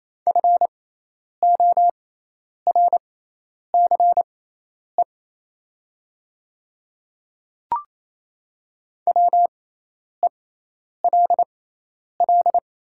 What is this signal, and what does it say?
Morse code, heard as a keyed tone, about 28 wpm, spaced as for 10 wpm, 690 Hz. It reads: FORCE WELL
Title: 6x Character Spacing - Top 500 Words - 28wpm